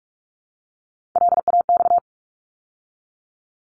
Morse code, heard as a keyed tone, about 46 wpm, 700 Hz. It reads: LAX
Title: ARRL Sections 45wpm